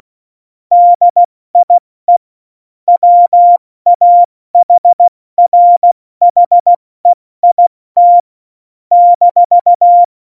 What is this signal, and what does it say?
Morse code, heard as a keyed tone, about 16 wpm, 700 Hz. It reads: DIE WAHRHEIT -
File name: Morsekurs / Witze / Witz 05. Januar 2021, 70 F80